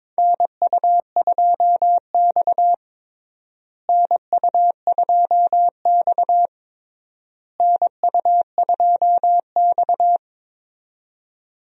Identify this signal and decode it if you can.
Morse code, heard as a keyed tone, about 22 wpm, 700 Hz. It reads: NU2X NU2X NU2X
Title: USA Call Signs - 4 Characters - Repeated 3x - 22wpm